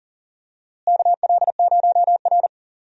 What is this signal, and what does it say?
Morse code, heard as a keyed tone, about 40 wpm, 700 Hz. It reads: KL0R
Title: USA Call Signs - 4 Characters - Repeated 2x - 40wpm